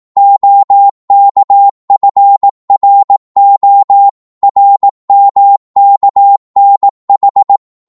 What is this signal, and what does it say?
Morse code, heard as a keyed tone, about 18 wpm, 800 Hz. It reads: OKFRO RMKNH